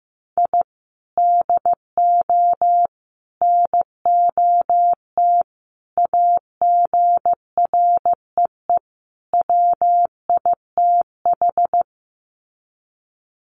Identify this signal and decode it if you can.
Morse code, heard as a keyed tone, about 15 wpm, 700 Hz. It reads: I DO NOT AGREE WITH
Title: Sets of 5 Words Encore 15wpm